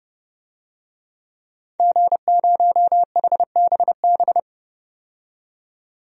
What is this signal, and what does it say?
Morse code, heard as a keyed tone, about 30 wpm, 700 Hz. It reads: G0HBB
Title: DXCC-100 Call Signs - 5 Characters - 30wpm